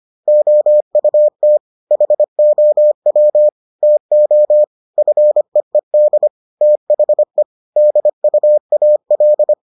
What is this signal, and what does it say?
Morse code, heard as a keyed tone, about 25 wpm, 600 Hz. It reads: OUT HOW TO FEED THE DUAL